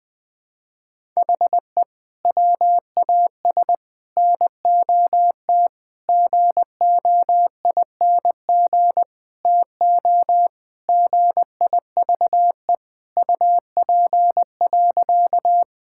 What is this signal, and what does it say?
Morse code, heard as a keyed tone, about 20 wpm, 700 Hz. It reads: HE WAS NOT GOING TO GIVE UP.
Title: Sentences with 4-Word Phrases - Spoken Beginning - 20wpm